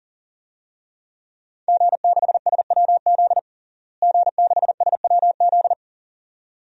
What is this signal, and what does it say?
Morse code, heard as a keyed tone, about 40 wpm, 690 Hz. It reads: G6SWZ G6SWZ
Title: DXCC-100 Call Signs - 5 Characters - Repeated 2x - 40wpm